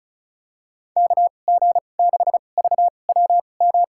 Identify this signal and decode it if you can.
Morse code, heard as a keyed tone, about 34 wpm, spaced as for 25 wpm, 700 Hz. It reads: KG6VWM
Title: USA Call Signs - 6 Characters - 2x Character Spacing - 35wpm